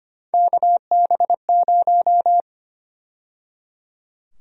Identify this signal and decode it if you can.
Morse code, heard as a keyed tone, about 25 wpm, 700 Hz. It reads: KB0
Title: USA Call Sign Prefixes - 25wpm